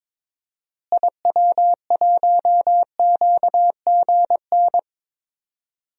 Morse code, heard as a keyed tone, about 22 wpm, 700 Hz. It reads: IW1QGN
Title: DXCC-100 Call Signs - 6 Characters - Repeated 2x - 22wpm